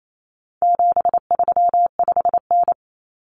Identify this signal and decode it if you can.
Morse code, heard as a keyed tone, about 28 wpm, 700 Hz. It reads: 735N
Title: Common Cut Numbers - 4-Digit 28wpm